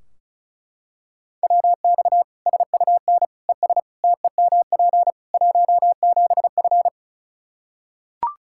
Morse code, heard as a keyed tone, about 34 wpm, 700 Hz. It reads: WX SUN ES TEMP 17F
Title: Practice QSO - Exchange 2 - Info - 35wpm